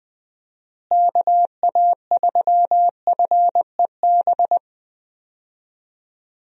Morse code, heard as a keyed tone, about 20 wpm, 700 Hz. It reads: KA3FEB
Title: USA Call Signs - 6 Characters - Repeated 3x - 20wpm